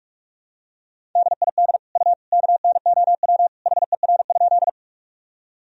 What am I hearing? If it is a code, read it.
Morse code, heard as a keyed tone, about 44 wpm, 700 Hz. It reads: DID U KNOW HER?
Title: Sentences from Top 100 Words and Abbreviations 45wpm